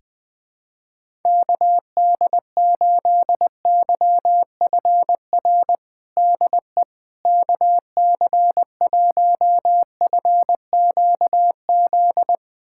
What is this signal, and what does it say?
Morse code, heard as a keyed tone, about 21 wpm, 700 Hz. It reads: KD8YFR DE KC1FQZ
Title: USA Callsign DE Callsign 20wpm